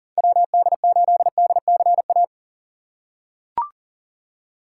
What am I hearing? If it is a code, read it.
Morse code, heard as a keyed tone, about 39 wpm, 700 Hz. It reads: WD8DCA
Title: USA Call Signs - 6 Characters - 4x Character Spacing - 40wpm